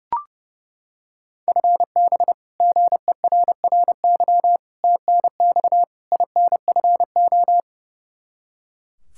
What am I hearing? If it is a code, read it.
Morse code, heard as a keyed tone, about 30 wpm, 700 Hz. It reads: FB GERRY TNX INFO
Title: Practice QSO - Exchange 3 - More Info - 30wpm